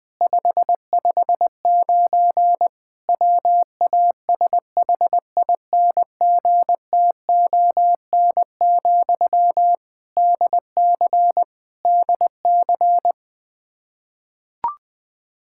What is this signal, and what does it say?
Morse code, heard as a keyed tone, about 21 wpm, 700 Hz. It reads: 559 WASHINGTON, DC DC
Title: Practice QSO - Exchange 1 - Beginning - 20wpm